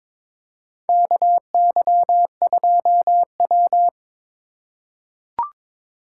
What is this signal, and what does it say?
Morse code, heard as a keyed tone, about 22 wpm, 700 Hz. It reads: KY2W
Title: USA Call Signs - 2x Character Spacing - 22wpm